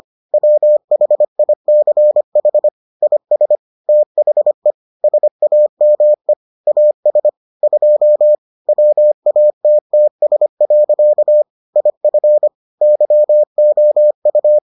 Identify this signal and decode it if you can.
Morse code, heard as a keyed tone, about 25 wpm, 600 Hz. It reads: WHICH IS THE SAME AS 2 WATTS. IF YOU